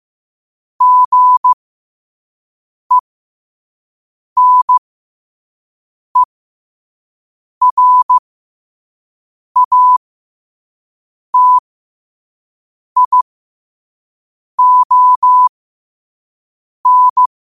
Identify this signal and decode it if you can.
Morse code, heard as a keyed tone, about 15 wpm, spaced as for 5 wpm, 1000 Hz. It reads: GENERATION
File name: qotdfiles / 5wpm